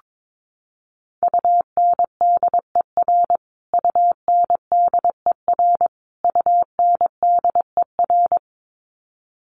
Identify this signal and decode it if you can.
Morse code, heard as a keyed tone, about 22 wpm, 700 Hz. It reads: UNDER UNDER UNDER
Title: Top 500 Words Repeated 22wpm